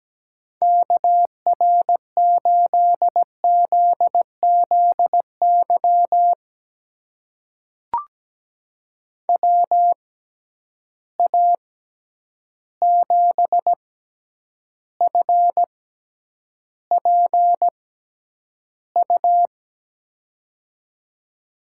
Morse code, heard as a keyed tone, about 17 wpm, spaced as for 9 wpm, 700 Hz. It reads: KR8ZZY WA7FPU
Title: USA Call Signs - 6 Characters - 6x Character Spacing - 17wpm